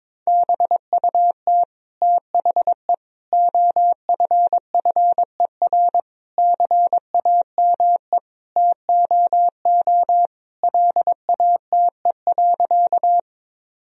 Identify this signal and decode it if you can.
Morse code, heard as a keyed tone, about 22 wpm, 700 Hz. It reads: BUT THE OFFER CAME TOO LATE.
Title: Sentences from Top 1000 Words - Review Part 4 - 22wpm